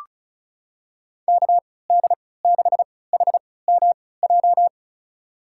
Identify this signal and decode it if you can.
Morse code, heard as a keyed tone, about 36 wpm, spaced as for 20 wpm, 700 Hz. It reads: KD6HMJ
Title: USA Call Signs - 6 Characters - 3x Character Spacing - 35wpm